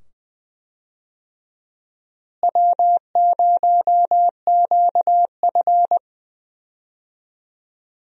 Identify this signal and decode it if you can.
Morse code, heard as a keyed tone, about 20 wpm, 700 Hz. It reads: W0QF
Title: USA Call Signs - 4 Characters - Repeated 2x - 20wpm